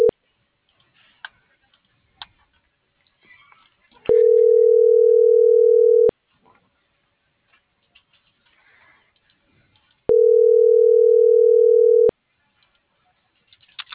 Ringback tone (North America), 440 + 480 Hz.
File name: CENTRAL MABE